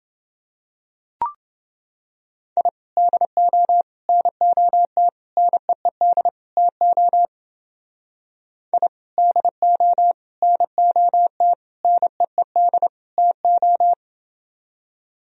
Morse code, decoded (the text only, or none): I DO NOT NEED TO I DO NOT NEED TO